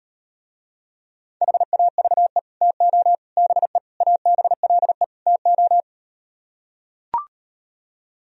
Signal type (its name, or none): Morse code